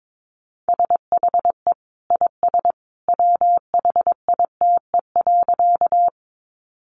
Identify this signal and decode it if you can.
Morse code, heard as a keyed tone, about 22 wpm, 700 Hz. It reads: SHE IS WHITE.
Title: Easy Phrases 22wpm